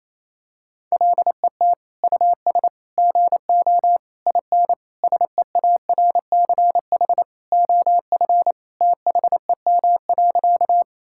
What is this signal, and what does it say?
Morse code, heard as a keyed tone, about 27 wpm, 700 Hz. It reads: LET US GO IN SEARCH OF THEM.
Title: Sentences from Top 1000 Words - Review Part 5 - 28wpm